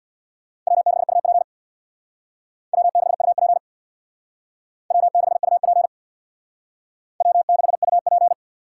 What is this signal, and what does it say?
Morse code, heard as a keyed tone, about 61 wpm, 700 Hz. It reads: W6UP W6UP W6UP W6UP
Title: Speed Racing - USA 1x2 Call Signs 50wpm